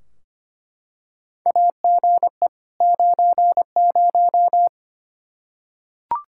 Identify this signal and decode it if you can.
Morse code, heard as a keyed tone, about 25 wpm, 700 Hz. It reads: AGE 90
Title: K1USN SST QSO Practice 25wpm